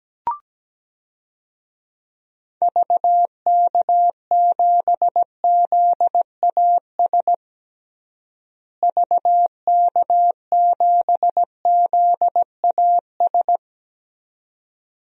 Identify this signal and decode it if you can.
Morse code, heard as a keyed tone, about 17 wpm, 700 Hz. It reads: VK7ZAS VK7ZAS
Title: DXCC-100 Call Signs - 6 Characters - Repeated 2x - 17wpm